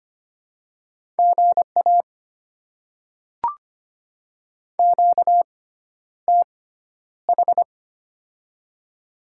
Morse code, heard as a keyed tone, about 25 wpm, spaced as for 11 wpm, 700 Hz. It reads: GA QTH
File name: Top 20 QSO Elements - 6x Character Spacing - 25wpm